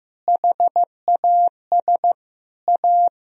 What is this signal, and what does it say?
Morse code, heard as a keyed tone, about 16 wpm, 700 Hz. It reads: HAS A
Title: Sentences from Top 1000 Words - Review Part 1 - 15wpm